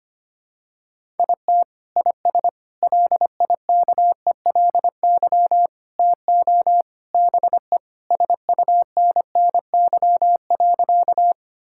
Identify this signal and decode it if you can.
Morse code, heard as a keyed tone, about 25 wpm, 700 Hz. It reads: IT IS LIKELY TO BE SUNNY.
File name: practice / 25wpm / Sentences with 3-Word Phrases - 25wpm